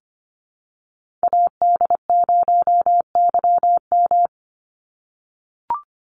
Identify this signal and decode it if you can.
Morse code, heard as a keyed tone, about 25 wpm, 700 Hz. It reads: AD0YM
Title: NA SOTA Chaser Callsigns 25wpm